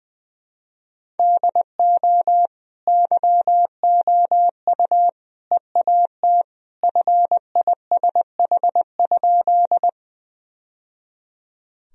Morse code, heard as a keyed tone, about 20 wpm, 700 Hz. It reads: DO YOU EAT FISH?
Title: Sentences from Top 1000 Words - Review Part 1 - 20wpm